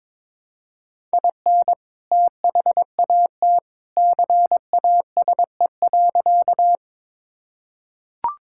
Morse code, heard as a keyed tone, about 22 wpm, 700 Hz. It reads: IN THAT CASE.
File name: practice / 22wpm / Sentences with 2-Word Phrases - 22wpm